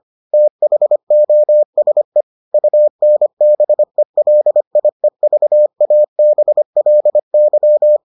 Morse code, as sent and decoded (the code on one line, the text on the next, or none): - .... --- ... / ..- -. -... .-.. .. ...- .- -... .-.. -.--
THOSE UNBELIEVABLY